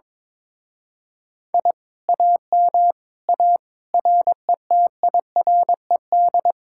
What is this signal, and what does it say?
Morse code, heard as a keyed tone, about 22 wpm, 700 Hz. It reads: I AM A RETIRED
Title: Speed-Racing Common QSO Phrases - 22wpm